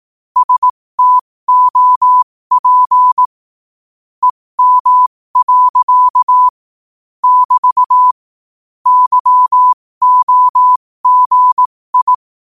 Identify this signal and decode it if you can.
Morse code, heard as a keyed tone, about 18 wpm, spaced as for 15 wpm, 1000 Hz. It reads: STOP EM. <BT> YOGI
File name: qotdfiles / 15wpm